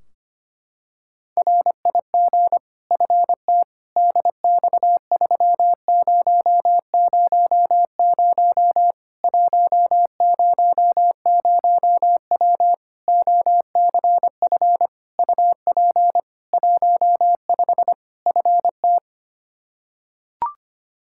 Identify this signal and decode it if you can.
Morse code, heard as a keyed tone, about 25 wpm, 700 Hz. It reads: RIG FT DX3000 100W OCF UP 15 FT
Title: K1USN SST QSO Practice 25wpm